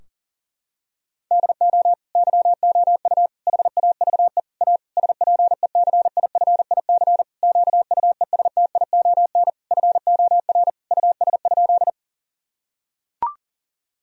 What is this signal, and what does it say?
Morse code, heard as a keyed tone, about 40 wpm, 700 Hz. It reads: DO YOU HAVE A SPECIFIC QUESTION FOR US?
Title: Sentences from Top 1000 Words - Review Part 3 - 40wpm